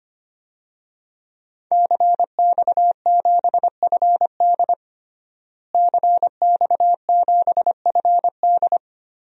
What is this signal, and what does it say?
Morse code, heard as a keyed tone, about 25 wpm, 700 Hz. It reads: CX7FD CX7FD
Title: DXCC-100 Call Signs - 5 Characters - Repeated 2x - 25wpm